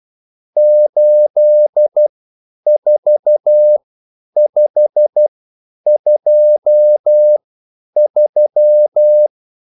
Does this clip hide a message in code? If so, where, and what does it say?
Yes: Morse code from 0.6 to 9.3 s, heard as a keyed tone, about 12 wpm, spaced as for 9 wpm, 600 Hz. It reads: 84523